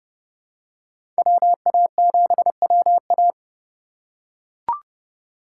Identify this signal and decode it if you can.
Morse code, heard as a keyed tone, about 30 wpm, 700 Hz. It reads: WA7WA
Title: USA Call Signs - 5 Characters - 3x Character Spacing - 30wpm